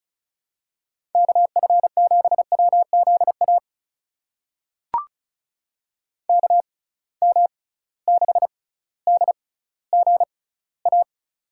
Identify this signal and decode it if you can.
Morse code, heard as a keyed tone, about 36 wpm, spaced as for 18 wpm, 700 Hz. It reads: KF7WZA KM6DGA
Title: USA Call Signs - 6 Characters - 6x Character Spacing - 35wpm